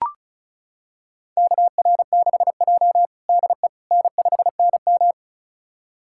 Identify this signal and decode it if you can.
Morse code, heard as a keyed tone, about 36 wpm, 700 Hz. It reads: KR6J DE N5NM
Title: USA Callsign DE Callsign - 4 Characters - 35wpm